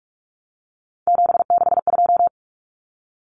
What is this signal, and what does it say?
Morse code, heard as a keyed tone, about 44 wpm, 700 Hz. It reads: 762
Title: Common Cut Numbers - 3-Digit 45wpm